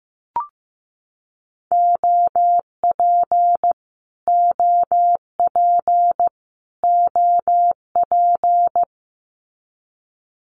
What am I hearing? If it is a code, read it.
Morse code, heard as a keyed tone, about 15 wpm, 700 Hz. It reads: OP OP OP
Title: Abbreviations Repeated 3x - 15wpm